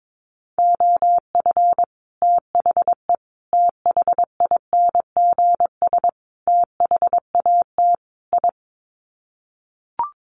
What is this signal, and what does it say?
Morse code, heard as a keyed tone, about 22 wpm, 700 Hz. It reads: OF THE THINGS THAT I
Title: Sets of 5 Words Encore 22wpm